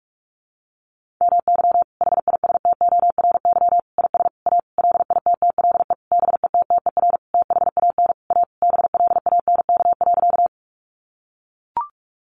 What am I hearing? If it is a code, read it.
Morse code, heard as a keyed tone, about 44 wpm, 700 Hz. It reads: MY HISTORY IS A LITTLE BETTER THAN A BLANK.
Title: Sentences from Top 1500 Words 45wpm